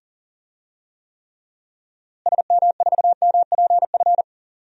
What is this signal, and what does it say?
Morse code, heard as a keyed tone, about 40 wpm, 700 Hz. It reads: SM4MPF
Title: DXCC-100 Call Signs - 6 Characters - Repeated 2x - 40wpm